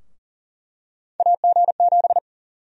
Morse code, heard as a keyed tone, about 40 wpm, 700 Hz. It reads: AG7
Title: USA Call Sign Prefixes - 40wpm